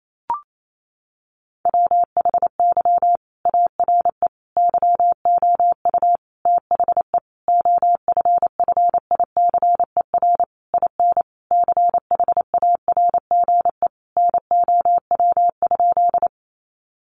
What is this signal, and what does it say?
Morse code, heard as a keyed tone, about 28 wpm, 700 Hz. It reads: WHY ARE YOU THE OFFICER IN CHARGE NOW?
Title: Sentences from Top 1200 Words 28wpm